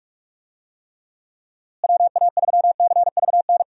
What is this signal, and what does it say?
Morse code, heard as a keyed tone, about 41 wpm, 700 Hz. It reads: WA3CVN
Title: USA Call Signs - 6 Characters - Repeated 3x - 45wpm